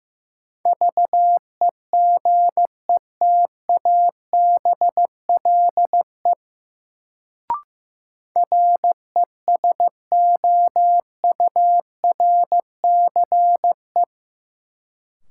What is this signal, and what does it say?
Morse code, heard as a keyed tone, about 15 wpm, 700 Hz. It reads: VEGETABLE RESOURCE
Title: Sentences from Top 1300 Words - New Word Drill Alt - 15wpm